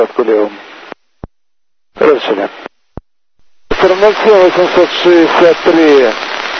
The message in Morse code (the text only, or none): none